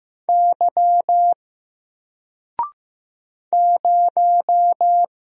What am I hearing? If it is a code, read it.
Morse code, heard as a keyed tone, about 15 wpm, 700 Hz. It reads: Y 0
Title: Single Character (Letter:Number:Punctuation:Prosign) 15wpm